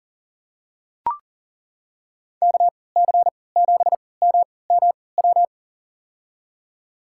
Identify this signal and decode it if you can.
Morse code, heard as a keyed tone, about 40 wpm, spaced as for 23 wpm, 700 Hz. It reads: KC7MMW
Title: USA Call Signs - 6 Characters - 3x Character Spacing - 40wpm